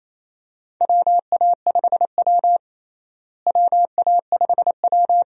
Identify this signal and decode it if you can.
Morse code, heard as a keyed tone, about 28 wpm, 700 Hz. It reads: WA5W WA5W
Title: USA Call Signs - 4 Characters - Repeated 2x - 28wpm